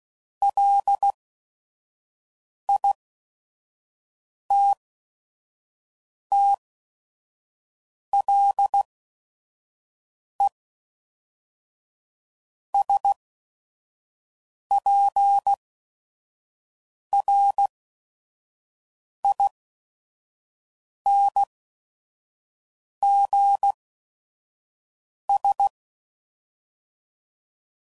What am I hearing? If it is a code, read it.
Morse code, heard as a keyed tone, about 16 wpm, spaced as for 5 wpm, 800 Hz. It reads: LITTLESPRINGS
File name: CW Code Groups @5WPM / fromSM's / Cities